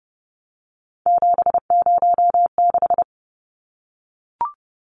Morse code, heard as a keyed tone, about 30 wpm, 700 Hz. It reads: 706
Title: Numbers - 3-Digit 30wpm